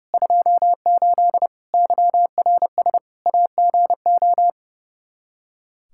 Morse code, heard as a keyed tone, about 32 wpm, 700 Hz. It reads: 28 YRS AGO
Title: K1USN SST QSO Practice 30wpm